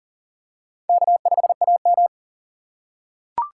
Morse code, heard as a keyed tone, about 40 wpm, 700 Hz. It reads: K5AM
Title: K1USN SST QSO Practice - 40wpm